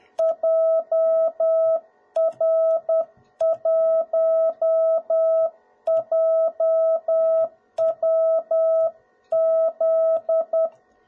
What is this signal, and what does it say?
Morse code, heard as a keyed tone, about 10 wpm, 650 Hz. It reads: JR1JWZ
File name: jh8cbh / 2021cw / cw01b